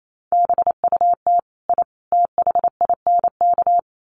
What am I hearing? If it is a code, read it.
Morse code, heard as a keyed tone, about 28 wpm, 700 Hz. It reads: BUT I THINK
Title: Sets of 3 Words 28wpm